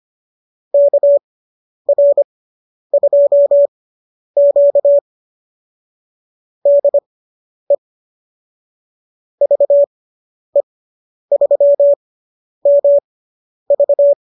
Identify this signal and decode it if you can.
Morse code, heard as a keyed tone, about 25 wpm, spaced as for 10 wpm, 590 Hz. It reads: KR2Q DE VE3MV